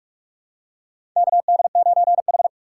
Morse code, heard as a keyed tone, about 44 wpm, 690 Hz. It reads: KD9H
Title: USA Call Signs - 4x Character Spacing - 45wpm